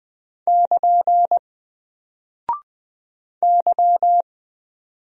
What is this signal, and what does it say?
Morse code, heard as a keyed tone, about 20 wpm, 700 Hz. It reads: <KN> Y
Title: Single Character (Letter:Number:Punctuation:Prosign) 20wpm